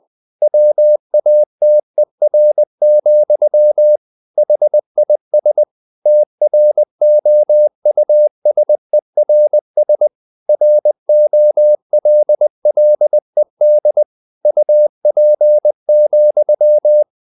Morse code, heard as a keyed tone, about 20 wpm, 600 Hz. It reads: WATER, HIS TROUSERS ROLLED UP,
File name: the old man and the sea part 2 20wpm